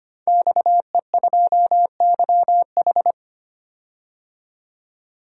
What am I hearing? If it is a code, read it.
Morse code, heard as a keyed tone, about 25 wpm, 700 Hz. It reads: XE2YH
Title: DXCC-100 Call Signs - 5 Characters - 25wpm